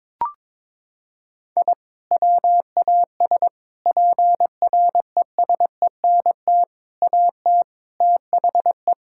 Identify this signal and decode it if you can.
Morse code, heard as a keyed tone, about 22 wpm, 700 Hz. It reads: I WAS PRESENT AT THE
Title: Sentences from Top 1000 Words - Review Part 5 - 22wpm